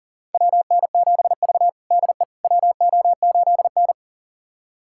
Morse code, heard as a keyed tone, about 40 wpm, 700 Hz. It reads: WN7V DE WO8N
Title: USA Callsign DE Callsign - 4 Characters - 40wpm